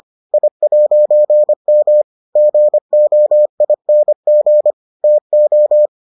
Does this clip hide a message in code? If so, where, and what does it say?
Yes: Morse code from 0.3 to 5.9 s, heard as a keyed tone, about 25 wpm, 600 Hz. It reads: I'M GOING TO